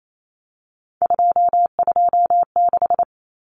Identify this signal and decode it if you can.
Morse code, heard as a keyed tone, about 28 wpm, 700 Hz. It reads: 226
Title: Numbers - 3-Digit 28wpm